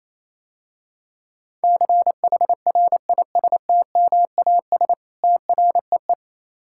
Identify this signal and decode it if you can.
Morse code, heard as a keyed tone, about 27 wpm, 700 Hz. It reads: CHRISTMAS TREE
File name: Easy Words - Speed-Racing with Hints - 22wpm